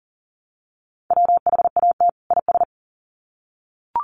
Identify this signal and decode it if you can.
Morse code, heard as a keyed tone, about 40 wpm, 700 Hz. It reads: WHAT IS
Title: Sets of 2 Words 40wpm